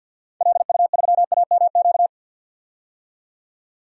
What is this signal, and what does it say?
Morse code, heard as a keyed tone, about 48 wpm, 700 Hz. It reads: LU3AMQ
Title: DXCC-100 Call Signs - 6 Characters - 50wpm